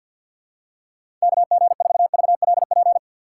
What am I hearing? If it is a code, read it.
Morse code, heard as a keyed tone, about 50 wpm, 700 Hz. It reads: KG4VLP
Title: USA Call Signs - 6 Characters - 4x Character Spacing - 50wpm